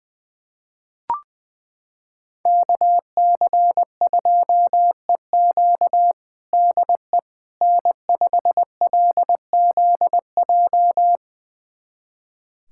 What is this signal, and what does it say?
Morse code, heard as a keyed tone, about 20 wpm, 700 Hz. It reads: KC2EQ DE N5LZJ
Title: USA Callsign DE Callsign - 5 Characters - 20wpm